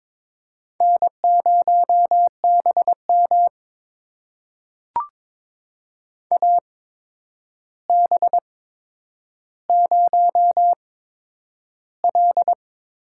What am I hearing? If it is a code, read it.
Morse code, heard as a keyed tone, about 22 wpm, spaced as for 9 wpm, 700 Hz. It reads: N0BM AB0L
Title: USA Call Signs - 8x Character Spacing - 22wpm